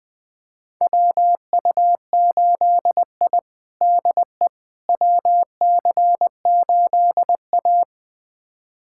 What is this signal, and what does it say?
Morse code, heard as a keyed tone, about 21 wpm, 700 Hz. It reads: WU8I DE WC8A
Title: USA Callsign DE Callsign - 4 Characters - 20wpm